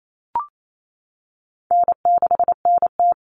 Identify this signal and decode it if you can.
Morse code, heard as a keyed tone, about 28 wpm, 700 Hz. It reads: N6NT
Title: Common Cut Numbers - 4-Digit 28wpm